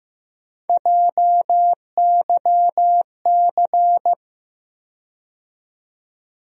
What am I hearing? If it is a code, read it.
Morse code, heard as a keyed tone, about 15 wpm, 700 Hz. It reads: JYC
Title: USA Call Sign Postfixes - 3 Letters 15wpm